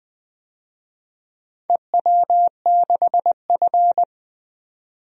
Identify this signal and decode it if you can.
Morse code, heard as a keyed tone, about 20 wpm, 700 Hz. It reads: EW6F
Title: DXCC-100 Call Signs - 4 Characters - 20wpm